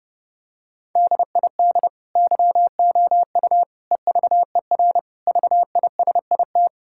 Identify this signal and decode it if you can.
Morse code, heard as a keyed tone, about 30 wpm, 700 Hz. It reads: DID YOU EVER VISIT
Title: Sentences from Top 1000 Words - Review Part 6 - 30wpm